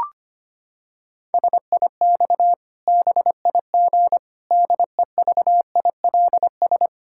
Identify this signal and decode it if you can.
Morse code, heard as a keyed tone, about 25 wpm, 700 Hz. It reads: SIX BIG DEVILS
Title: Pangram Challenge v2 - 25wpm